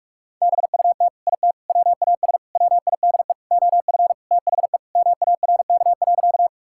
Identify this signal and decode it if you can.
Morse code, heard as a keyed tone, about 41 wpm, 700 Hz. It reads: BUT IT WAS WIDE OF THE MARK.
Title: Sentences from Top 1000 Words - Review Part 6 - 45wpm